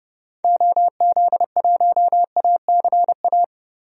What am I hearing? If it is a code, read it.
Morse code, heard as a keyed tone, about 30 wpm, 700 Hz. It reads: OZ1ACA
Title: DXCC-100 Call Signs - 6 Characters - Repeated 3x - 30wpm